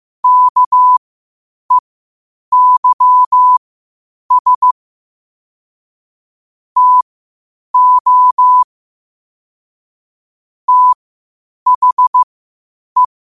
Morse code, heard as a keyed tone, about 15 wpm, spaced as for 8 wpm, 1000 Hz. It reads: KEYS TO THE
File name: qotdfiles / 8wpm